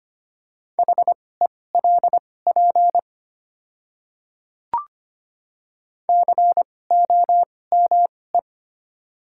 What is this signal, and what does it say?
Morse code, heard as a keyed tone, about 25 wpm, spaced as for 18 wpm, 700 Hz. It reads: HELP COME
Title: 2x Character Spacing - Top 200 Words - 25wpm